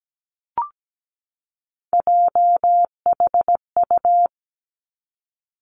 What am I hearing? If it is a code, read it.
Morse code, heard as a keyed tone, about 17 wpm, 700 Hz. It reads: JHU
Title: Three Challenging Letter-Number Combinations 17wpm